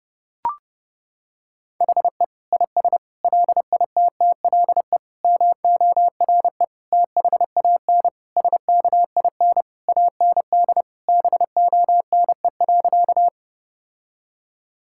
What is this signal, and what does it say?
Morse code, heard as a keyed tone, about 30 wpm, 700 Hz. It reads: HE IS LITTLE MORE THAN SKIN AND BONE.
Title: Sentences from Top 1600 Words 30wpm